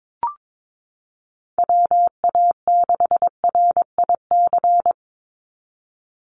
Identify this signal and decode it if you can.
Morse code, heard as a keyed tone, about 22 wpm, 700 Hz. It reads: WA6RIC
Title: NA SOTA Chaser Callsigns 22wpm